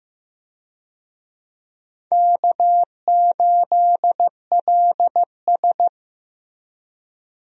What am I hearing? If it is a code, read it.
Morse code, heard as a keyed tone, about 15 wpm, 700 Hz. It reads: K8LS
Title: USA Call Signs - 4 Characters - Repeated 2x - 15wpm